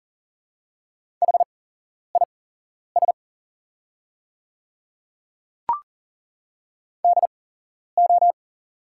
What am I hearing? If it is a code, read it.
Morse code, heard as a keyed tone, about 40 wpm, spaced as for 11 wpm, 700 Hz. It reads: HIS DO
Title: 8x Character Spacing - Top 100 Words - 40wpm